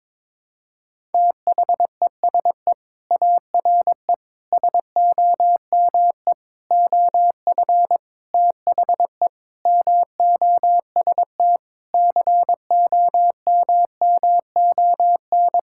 Morse code, heard as a keyed tone, about 22 wpm, 700 Hz. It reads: THESE ARE SOME OF THE MOST COMMON
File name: practice / 22wpm / Sentences with 4-Word Phrases - 22wpm